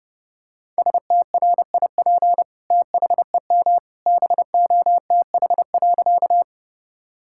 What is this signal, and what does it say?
Morse code, heard as a keyed tone, about 32 wpm, 700 Hz. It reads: STRIP THEM BOTH.
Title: Sentences from Top 1600 Words 30wpm